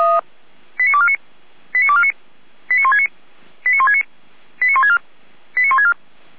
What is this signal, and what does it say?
Five-tone selective call (ZVEI), heard as quick runs of five tones, tones of about 70 ms. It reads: 89249 89249 89169 89169 89164 89164
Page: /455/SDR/ZVEI/89167/